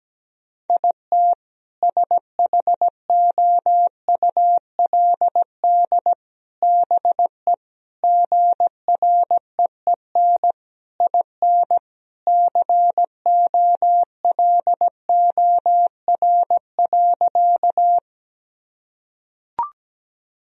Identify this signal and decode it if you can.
Morse code, heard as a keyed tone, about 17 wpm, 700 Hz. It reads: IT SHOULD BE GREEN IN COLOR.
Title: Sentences from Top 1000 Words - Review Part 2 - 17wpm